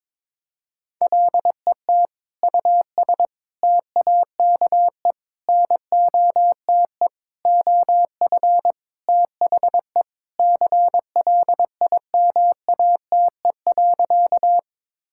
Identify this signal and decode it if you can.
Morse code, heard as a keyed tone, about 22 wpm, 700 Hz. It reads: LET US TAKE NOTE OF THE CLIMATE.
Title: Sentences from Top 1600 Words 22wpm